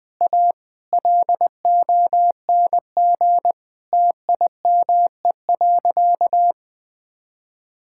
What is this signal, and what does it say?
Morse code, heard as a keyed tone, about 20 wpm, 700 Hz. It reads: A LONG TIME.
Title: Sentences with 7-Word Phrases - 20wpm